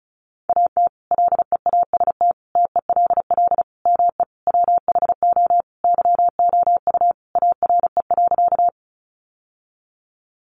Morse code, heard as a keyed tone, about 34 wpm, 700 Hz. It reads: AT LEAST TELL ME WHO YOU ARE.
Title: Sentences from Top 700 Words 35wpm